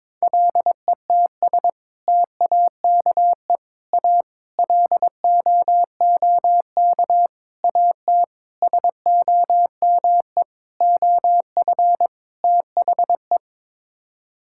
Morse code, heard as a keyed tone, about 22 wpm, 700 Hz. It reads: LETS TAKE A LOOK AT SOME OF THE
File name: practice / 22wpm / Sets of 8 Words - 22wpm